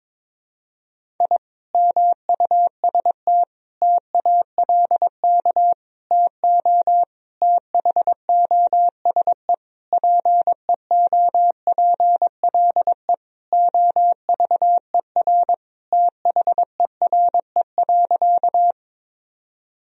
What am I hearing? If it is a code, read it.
Morse code, heard as a keyed tone, about 22 wpm, 700 Hz. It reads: I MUST TALK TO THOSE PEOPLE OVER THERE.
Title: Sentences from Top 1000 Words - Review Part 1 - 22wpm